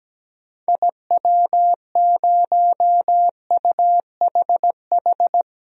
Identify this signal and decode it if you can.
Morse code, heard as a keyed tone, about 17 wpm, 700 Hz. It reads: IW0UHH
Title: DXCC-100 Call Signs - 6 Characters - 17wpm